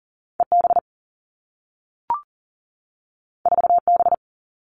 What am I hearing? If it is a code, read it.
Morse code, heard as a keyed tone, about 40 wpm, 700 Hz. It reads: EB 4B
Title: Two Challenging Letter-Number Combinations 40wpm